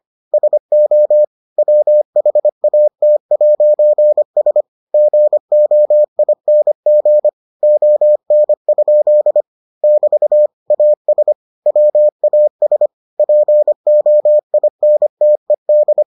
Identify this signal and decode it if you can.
Morse code, heard as a keyed tone, about 25 wpm, 600 Hz. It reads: SO WHAT'S GOING ON? <BT> AS WAS POINTED